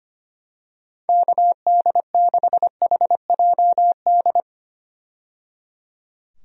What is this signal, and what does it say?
Morse code, heard as a keyed tone, about 25 wpm, 700 Hz. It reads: KD6HJD
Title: USA Call Signs - 6 Characters - 25wpm